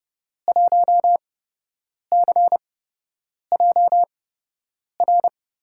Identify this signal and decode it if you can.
Morse code, heard as a keyed tone, about 30 wpm, spaced as for 8 wpm, 700 Hz. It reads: 1CJR
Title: USA Call Signs - 5 Characters - 8x Character Spacing - 30wpm